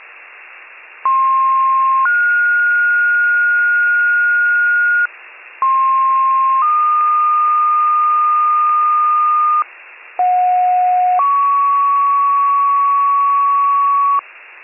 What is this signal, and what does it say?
Two-tone page, heard as a one-second tone, then a three-second tone, about 1060 then 1430 Hz, about 1060 then 1220 Hz, about 730 then 1120 Hz. It is heard from 1.0 to 14.2 s.